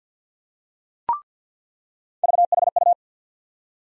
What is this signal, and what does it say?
Morse code, heard as a keyed tone, about 48 wpm, 700 Hz. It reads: VHU
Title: USA Call Sign Postfixes - 3 Letters 50wpm